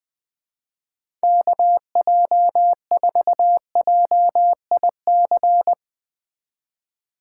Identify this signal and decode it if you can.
Morse code, heard as a keyed tone, about 20 wpm, 700 Hz. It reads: KJ4JIC
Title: USA Call Signs - 6 Characters - 2x Character Spacing - 20wpm